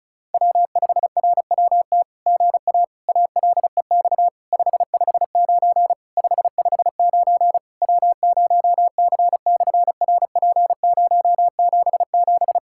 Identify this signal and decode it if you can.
Morse code, heard as a keyed tone, about 36 wpm, 700 Hz. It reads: W5RWT GA ALEX 559 559 W0C/RP077
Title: SOTA QSO Practice 35wpm